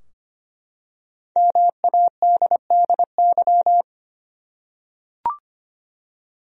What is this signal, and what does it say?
Morse code, heard as a keyed tone, about 25 wpm, 700 Hz. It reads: MADDY
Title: USA Common Names - Max 5 Letters - 25wpm